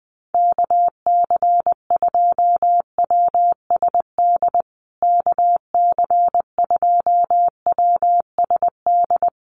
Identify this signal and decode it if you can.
Morse code, heard as a keyed tone, about 20 wpm, 700 Hz. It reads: KC2WSD KC2WSD
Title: USA Call Signs Repeated 3x 20wpm